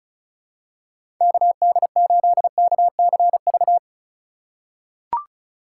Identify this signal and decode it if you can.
Morse code, heard as a keyed tone, about 36 wpm, 700 Hz. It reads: KD8KCV
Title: USA Call Signs - 6 Characters - 3x Character Spacing - 35wpm